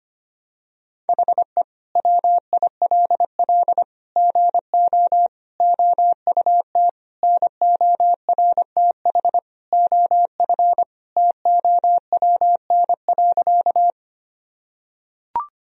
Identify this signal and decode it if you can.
Morse code, heard as a keyed tone, about 25 wpm, 700 Hz. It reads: HE WILL GO OUT NORTH OF TOWN.
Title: Sentences from Top 1000 Words - Review Part 1 - 25wpm